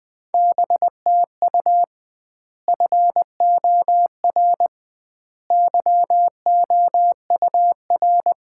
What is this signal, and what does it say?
Morse code, heard as a keyed tone, about 20 wpm, 700 Hz. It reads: BTU FOR YOUR
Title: Common QSO Phrases - 2x Word-Spacing - 20wpm